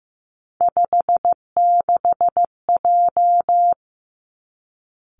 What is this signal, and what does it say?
Morse code, heard as a keyed tone, about 15 wpm, 700 Hz. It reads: 56J
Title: Three Challenging Letter-Number Combinations 15wpm